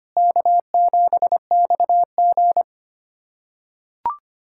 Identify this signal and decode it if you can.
Morse code, heard as a keyed tone, about 25 wpm, 700 Hz. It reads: K7XG
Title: USA Callsign DE Callsign - 4 Characters - 25wpm